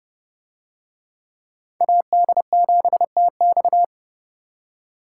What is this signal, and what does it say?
Morse code, heard as a keyed tone, about 29 wpm, 700 Hz. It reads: AD7TX